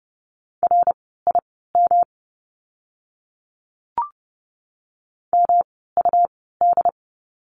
Morse code, heard as a keyed tone, about 30 wpm, spaced as for 17 wpm, 700 Hz. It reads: RIM MUD
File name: course / 30:10wpm / 108 Lesson 18f - Letter m - Words 30wpm